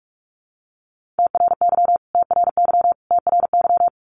Morse code, heard as a keyed tone, about 44 wpm, 700 Hz. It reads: TRY TRY TRY
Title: Top 500 Words Repeated 45wpm